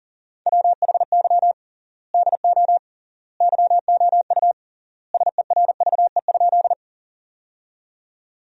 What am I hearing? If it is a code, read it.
Morse code, heard as a keyed tone, about 37 wpm, 700 Hz. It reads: WHY DO YOU SERVE?
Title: 3x Word-Spacing - Sentences from Top 400 Words - 40wpm